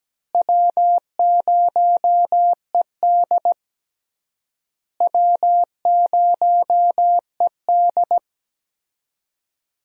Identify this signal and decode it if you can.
Morse code, heard as a keyed tone, about 17 wpm, 700 Hz. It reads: W0ED W0ED
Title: USA Call Signs - 4 Characters - Repeated 3x - 17wpm